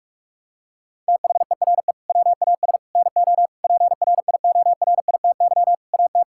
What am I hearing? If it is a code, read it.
Morse code, heard as a keyed tone, about 41 wpm, 700 Hz. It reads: THERE WAS NO PRIORITY AT
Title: Sentences from Top 1600 Words 45wpm